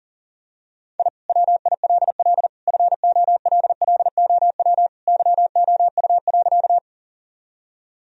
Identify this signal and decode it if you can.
Morse code, heard as a keyed tone, about 37 wpm, 700 Hz. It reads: I WILL FOLLOW YOU.
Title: Sentences from Top 1000 Words - Review Part 1 - 40wpm